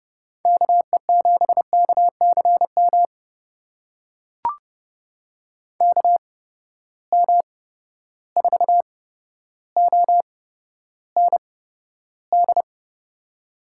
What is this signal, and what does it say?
Morse code, heard as a keyed tone, about 32 wpm, spaced as for 13 wpm, 700 Hz. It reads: KE7KCM KM4OND
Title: USA Call Signs - 6 Characters - 8x Character Spacing - 30wpm